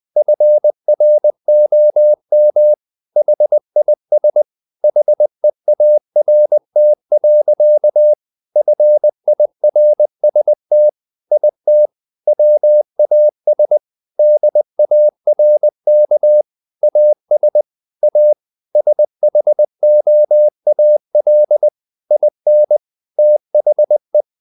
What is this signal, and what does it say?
Morse code, heard as a keyed tone, about 20 wpm, 600 Hz. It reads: FROM HIS HEART. FIRST IT WAS DARK AS A SHOAL IN THE
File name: the old man and the sea part 2 20wpm